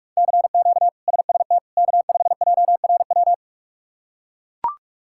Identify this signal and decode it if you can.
Morse code, heard as a keyed tone, about 42 wpm, 700 Hz. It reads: CQ SST K5JRW
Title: K1USN SST QSO Practice - 45wpm